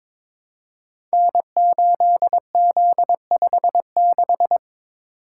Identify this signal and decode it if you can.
Morse code, heard as a keyed tone, about 22 wpm, 700 Hz. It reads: N8Z56
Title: Code Groups - 5 Letter:Numbers - 22wpm